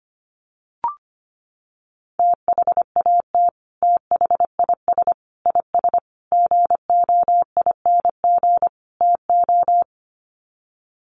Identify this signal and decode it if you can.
Morse code, heard as a keyed tone, about 25 wpm, 700 Hz. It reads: THAT THIS IS GOING TO
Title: Sets of 5 Words Encore2 25wpm